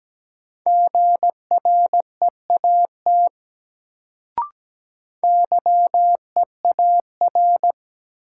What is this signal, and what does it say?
Morse code, heard as a keyed tone, about 17 wpm, 700 Hz. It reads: GREAT YEAR
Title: Sentences from Top 200 Words - New Word Drill - 17wpm